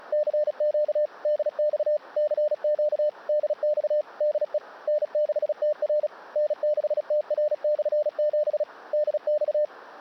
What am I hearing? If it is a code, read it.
Morse code, heard as a keyed tone, about 37 wpm, 600 Hz. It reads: CQ DX CQ DX DE N6TR N6TR/7 DX
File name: 35 30 WPM Remote